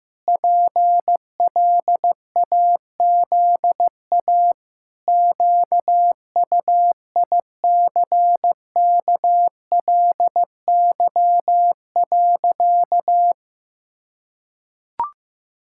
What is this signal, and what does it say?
Morse code, heard as a keyed tone, about 16 wpm, 700 Hz. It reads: PLAZA QUICKLY.
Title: Pangram Challenge v2 - 15wpm